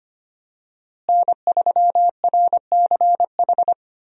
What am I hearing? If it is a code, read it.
Morse code, heard as a keyed tone, about 25 wpm, 700 Hz. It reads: N3RCH
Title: USA Call Signs - 5 Characters - Repeated 3x - 25wpm